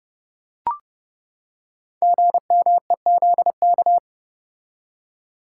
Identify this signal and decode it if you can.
Morse code, heard as a keyed tone, about 30 wpm, 700 Hz. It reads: GMEZK
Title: Code Groups - 5 Letter:Numbers - 30wpm